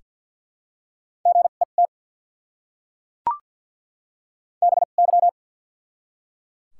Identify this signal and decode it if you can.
Morse code, heard as a keyed tone, about 48 wpm, spaced as for 35 wpm, 700 Hz. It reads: GET BY
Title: 2x Character Spacing - Top 500 Words - 50wpm